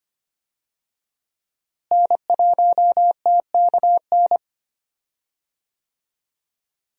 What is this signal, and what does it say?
Morse code, heard as a keyed tone, about 25 wpm, 700 Hz. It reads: N1TKN
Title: USA Call Signs - 5 Characters - Repeated 2x - 25wpm